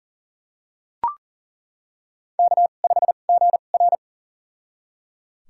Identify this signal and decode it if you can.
Morse code, heard as a keyed tone, about 37 wpm, spaced as for 28 wpm, 700 Hz. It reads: K5GR
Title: USA Call Signs - 4 Characters - 2x Character Spacing - 40wpm